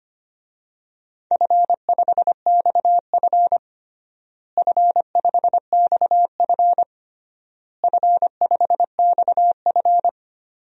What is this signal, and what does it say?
Morse code, heard as a keyed tone, about 25 wpm, 700 Hz. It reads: F5XF F5XF F5XF
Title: DXCC-100 Call Signs - 4 Characters - Repeated 3x - 25wpm